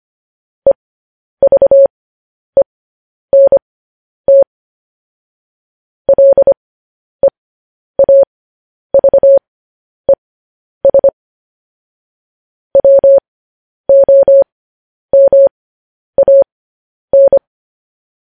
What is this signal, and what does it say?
Morse code, heard as a keyed tone, about 25 wpm, spaced as for 10 wpm, 550 Hz. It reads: EVENT LEAVES WOMAN
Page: news headlines in morse code at 10 wpm